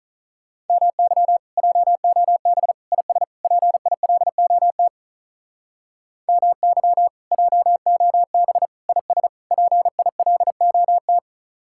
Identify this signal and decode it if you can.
Morse code, heard as a keyed tone, about 44 wpm, 700 Hz. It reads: MY JOB IS PILOT MY JOB IS PILOT
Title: Speed-Racing Common QSO Phrases - 35wpm